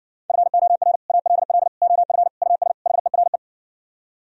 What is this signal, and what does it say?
Morse code, heard as a keyed tone, about 59 wpm, 700 Hz. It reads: FOR ALL OF US HERE